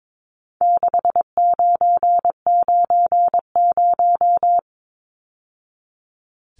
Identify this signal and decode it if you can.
Morse code, heard as a keyed tone, about 22 wpm, 700 Hz. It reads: 6990
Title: Numbers - 4-Digit 22wpm